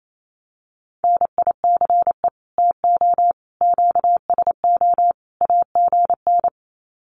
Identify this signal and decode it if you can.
Morse code, heard as a keyed tone, about 28 wpm, 700 Hz. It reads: NICE TO QSO AGN
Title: Common QSO Phrases 28wpm